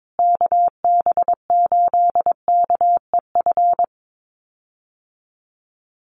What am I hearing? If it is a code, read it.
Morse code, heard as a keyed tone, about 22 wpm, 700 Hz. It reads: KB8KEF
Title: USA Call Signs Repeated 3x 22wpm